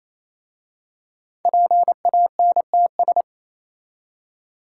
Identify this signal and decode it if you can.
Morse code, heard as a keyed tone, about 27 wpm, 700 Hz. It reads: PANTS
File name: Easy Words - Speed-Racing with Hints - 22wpm